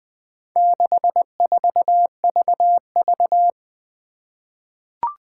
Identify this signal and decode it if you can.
Morse code, heard as a keyed tone, about 20 wpm, 700 Hz. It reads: 64VV
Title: Four Challenging Letter-Number Combinations 20wpm